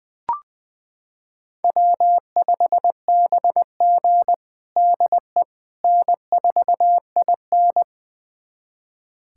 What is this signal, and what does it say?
Morse code, heard as a keyed tone, about 20 wpm, 700 Hz. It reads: W5BG DE N4IN
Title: USA Callsign DE Callsign - 4 Characters - 20wpm